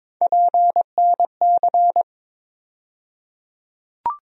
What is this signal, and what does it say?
Morse code, heard as a keyed tone, about 21 wpm, 700 Hz. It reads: PNC
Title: USA Call Sign Postfixes - 3 Letters 22wpm